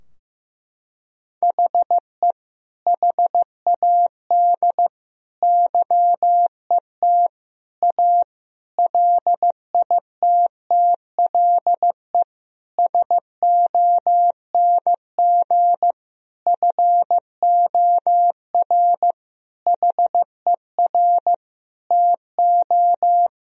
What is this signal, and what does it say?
Morse code, heard as a keyed tone, about 15 wpm, 700 Hz. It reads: HE HAD YET A LITTLE SONG FOR HER TO
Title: Sentences from Top 1000 Words - Review Part 2 - 15wpm